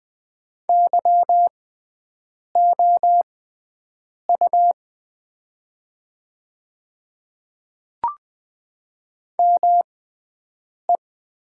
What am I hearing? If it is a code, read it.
Morse code, heard as a keyed tone, about 20 wpm, spaced as for 7 wpm, 700 Hz. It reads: YOU ME
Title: 6x Character Spacing - Top 200 Words - 20wpm